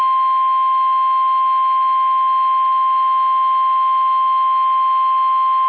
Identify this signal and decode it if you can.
Emergency Alert System (SAME) tones (the 1050 Hz attention tone). It is heard from 0.9 to 5.1 s.
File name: eas / KUSO / KUSO